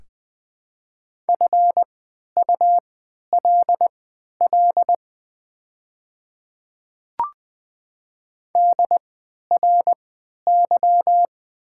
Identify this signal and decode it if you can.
Morse code, heard as a keyed tone, about 21 wpm, spaced as for 11 wpm, 700 Hz. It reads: FULL DRY
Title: 3x Character Spacing - Top 500 Words - 20wpm